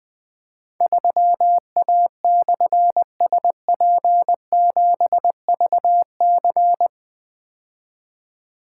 Morse code, heard as a keyed tone, about 20 wpm, 700 Hz. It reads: 3A/SP7VC
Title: Difficult Call Signs 20wpm